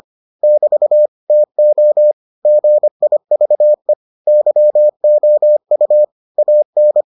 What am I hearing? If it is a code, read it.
Morse code, heard as a keyed tone, about 25 wpm, 600 Hz. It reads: <BT> TO GIVE YOU AN